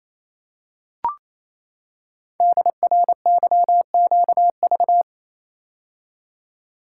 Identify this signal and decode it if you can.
Morse code, heard as a keyed tone, about 29 wpm, 700 Hz. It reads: DRYQV
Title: Code Groups - 5 Letters - 28wpm